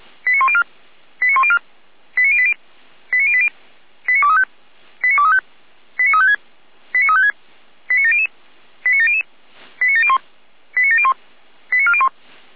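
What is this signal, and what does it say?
Five-tone selective call (ZVEI), heard as quick runs of five tones, tones of about 70 ms. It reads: 89104 89104 89080 89080 89235 89235 89356 89356 89700 89700 89701 89701 89401
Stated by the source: ZVEI five-tone selective call 89115, received by SDR.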